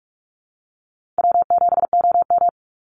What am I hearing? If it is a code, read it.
Morse code, heard as a keyed tone, about 44 wpm, 700 Hz. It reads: W7OM
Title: Easy Call Signs Repeated 2x - 45wpm